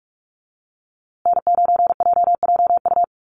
Morse code, heard as a keyed tone, about 44 wpm, 700 Hz. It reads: N9JJU
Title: USA Call Signs Repeated 3x 45wpm